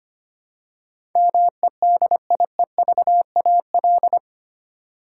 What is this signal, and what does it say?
Morse code, heard as a keyed tone, about 25 wpm, 700 Hz. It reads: MEDIEVAL
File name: 8 Letter Words - First 3 Letters Given - 25wpm